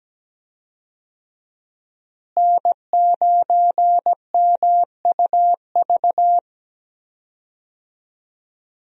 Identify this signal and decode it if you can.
Morse code, heard as a keyed tone, about 17 wpm, 700 Hz. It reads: N9MUV
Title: USA Call Signs - 5 Characters - Repeated 2x - 17wpm